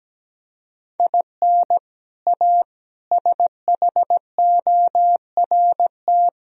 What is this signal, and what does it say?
Morse code, heard as a keyed tone, about 17 wpm, 700 Hz. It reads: IN A SHORT
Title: Sentences from Top 1000 Words - Review Part 2 - 17wpm